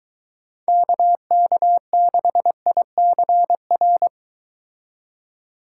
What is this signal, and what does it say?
Morse code, heard as a keyed tone, about 23 wpm, 700 Hz. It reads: KK6ICR